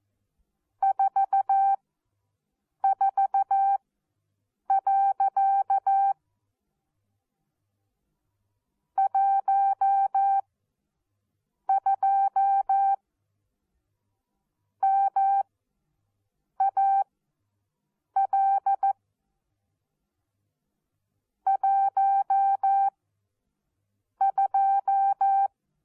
Morse code, heard as a keyed tone, about 14 wpm, spaced as for 6 wpm, 780 Hz. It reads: 44. 12 MAL 12
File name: Onlinelehrgang / morsen / DJ4UF Morsen14